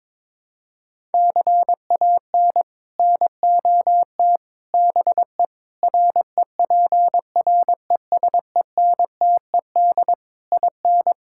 Morse code, heard as a keyed tone, about 23 wpm, 700 Hz. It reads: CAN NOT BE REPRESENTED IN
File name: practice / 22wpm / Build A Phrase - 5-Word Phrases - 22wpm